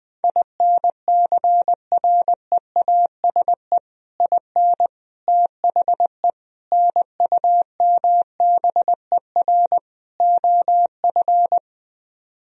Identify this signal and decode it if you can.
Morse code, heard as a keyed tone, about 21 wpm, 700 Hz. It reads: INCREASE IN THE NUMBER OF
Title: Speed-Racing - Sets of 5 Words 20wpm